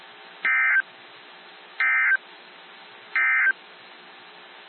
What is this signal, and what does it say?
Emergency Alert System (SAME) tones, heard as harsh buzzing bursts. It is heard from 0.3 to 3.6 s.
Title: NOAA Weather Radio - WXJ73